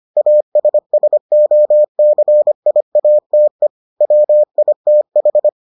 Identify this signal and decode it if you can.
Morse code, heard as a keyed tone, about 25 wpm, 600 Hz. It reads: ASSOCIATE WITH